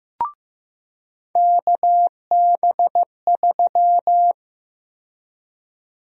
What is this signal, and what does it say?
Morse code, heard as a keyed tone, about 15 wpm, 700 Hz. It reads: KB3
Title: USA Call Sign Prefixes - 15wpm